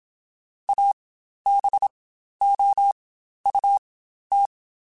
Morse code, heard as a keyed tone, about 27 wpm, spaced as for 12 wpm, 800 Hz. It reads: ABOUT
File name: Spaced Characters / Spaced Characters-mp3s / Weather4 24 sp @25